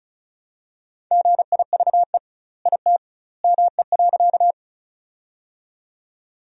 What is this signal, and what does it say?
Morse code, heard as a keyed tone, about 34 wpm, 700 Hz. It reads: GIVE IT ME.
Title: 2x Word-Spacing - Sentences from Top 200 Words - 35wpm